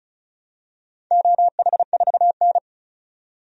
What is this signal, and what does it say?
Morse code, heard as a keyed tone, about 34 wpm, 700 Hz. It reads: OH4N
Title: DXCC-100 Call Signs - 4 Characters - 35wpm